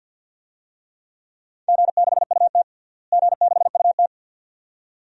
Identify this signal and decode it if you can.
Morse code, heard as a keyed tone, about 51 wpm, 700 Hz. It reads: G6UT G6UT
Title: DXCC-100 Call Signs - 4 Characters - Repeated 2x - 50wpm